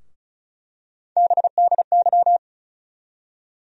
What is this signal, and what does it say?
Morse code, heard as a keyed tone, about 36 wpm, 700 Hz. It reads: BDY
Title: USA Call Sign Postfixes - 3 Letters 35wpm